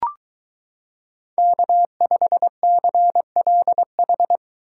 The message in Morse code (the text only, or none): K5CLH